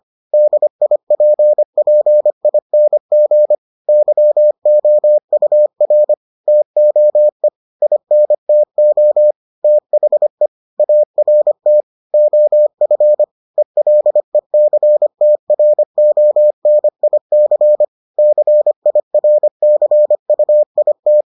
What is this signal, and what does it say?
Morse code, heard as a keyed tone, about 25 wpm, 600 Hz. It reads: DIPPING YOUR TOE INTO THE ART OF ELECTRONIC CIRCUIT